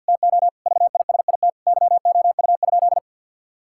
Morse code, heard as a keyed tone, about 50 wpm, 700 Hz. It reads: TO VISIT YOU?
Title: Sentences from Top 1000 Words - Review Part 6 - 50wpm